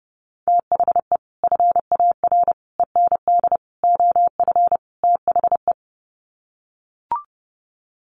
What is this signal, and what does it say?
Morse code, heard as a keyed tone, about 30 wpm, 700 Hz. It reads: THE FAR END OF THE
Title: Sets of 5 Words 30wpm